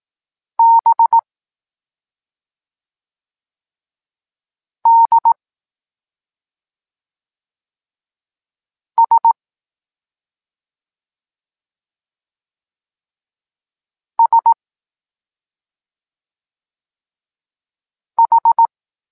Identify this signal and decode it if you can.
Morse code, heard as a keyed tone, about 18 wpm, 920 Hz. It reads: B D S S H
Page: Tekst 14 – doskonalenie odbioru poznanych liter